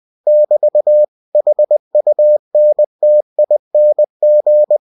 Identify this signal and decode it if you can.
Morse code, heard as a keyed tone, about 20 wpm, 600 Hz. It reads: <BT> HUNTING
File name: winnie-20wpm